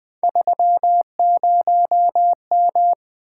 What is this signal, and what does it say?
Morse code, heard as a keyed tone, about 20 wpm, 700 Hz. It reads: 30M
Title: Three Letter-Number Combinations - 20wpm